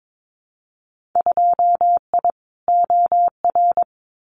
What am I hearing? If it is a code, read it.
Morse code, heard as a keyed tone, about 22 wpm, 700 Hz. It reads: 2I OR
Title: Winter Field Day Exchanges Encore 22wpm